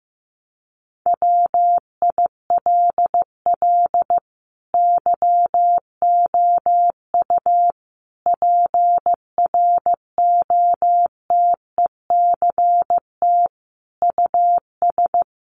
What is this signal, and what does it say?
Morse code, heard as a keyed tone, about 15 wpm, 700 Hz. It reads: WILL YOU PROTECT US
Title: Sentences from Top 900 Words 15wpm